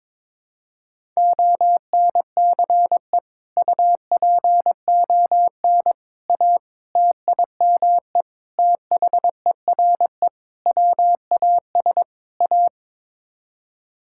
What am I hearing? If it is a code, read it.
Morse code, heard as a keyed tone, about 22 wpm, 700 Hz. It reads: ONCE UPON A TIME THERE WAS A
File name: practice / 22wpm / Sets of 7 Words - 22wpm